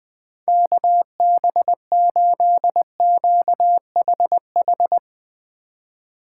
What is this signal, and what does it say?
Morse code, heard as a keyed tone, about 19 wpm, 700 Hz. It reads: KB8QHH
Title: Speed Racing - USA 2x3 Call Signs 20wpm